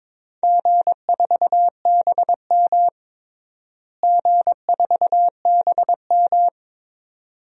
Morse code, heard as a keyed tone, about 22 wpm, 700 Hz. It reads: G4BM G4BM
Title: DXCC-100 Call Signs - 4 Characters - Repeated 2x - 22wpm